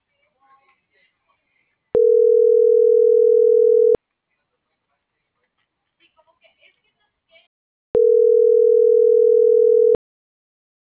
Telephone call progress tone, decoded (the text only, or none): ringback tone